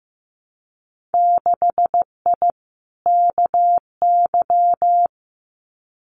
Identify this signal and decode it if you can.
Morse code, heard as a keyed tone, about 15 wpm, 700 Hz. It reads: 6I KY
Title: Winter Field Day Exchanges Encore 15wpm